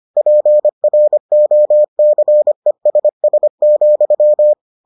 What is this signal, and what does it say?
Morse code, heard as a keyed tone, about 25 wpm, 600 Hz. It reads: PROCESS,